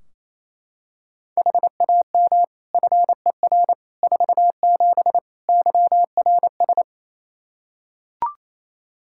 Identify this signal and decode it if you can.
Morse code, heard as a keyed tone, about 28 wpm, 700 Hz. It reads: HAM FER 47 YRS
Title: K1USN SST QSO Practice 28wpm